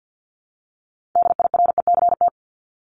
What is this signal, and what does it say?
Morse code, heard as a keyed tone, about 50 wpm, 700 Hz. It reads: DIRECT
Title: LY Suffix - Top 100 Words with Base Words 50wpm